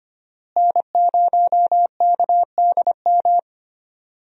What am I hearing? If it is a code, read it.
Morse code, heard as a keyed tone, about 25 wpm, 700 Hz. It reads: N0KDM
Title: USA Call Signs - 5 Characters - Repeated 2x - 25wpm